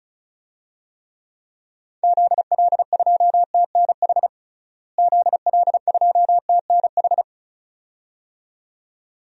Morse code, heard as a keyed tone, about 34 wpm, 700 Hz. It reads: ZL2TNH ZL2TNH
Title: DXCC-100 Call Signs - 6 Characters - Repeated 2x - 35wpm